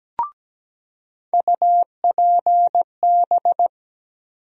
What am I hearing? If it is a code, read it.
Morse code, heard as a keyed tone, about 17 wpm, 700 Hz. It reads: UPB
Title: USA Call Sign Postfixes - 3 Letters 17wpm